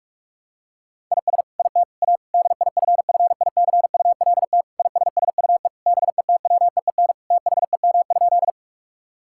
Morse code, heard as a keyed tone, about 46 wpm, 700 Hz. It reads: IS IT A DIFFICULT ISSUE BETWEEN THEM?
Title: Sentences from Top 1000 Words - Review Part 6 - 45wpm